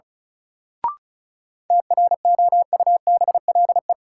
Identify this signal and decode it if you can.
Morse code, heard as a keyed tone, about 36 wpm, 700 Hz. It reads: TROUBLE
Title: Sentences from Top 1200 Words - New Word Drill - 35wpm